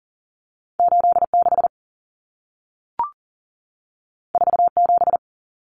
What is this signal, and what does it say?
Morse code, heard as a keyed tone, about 40 wpm, 700 Hz. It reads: 86 47
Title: Numbers - 2-Digit 40wpm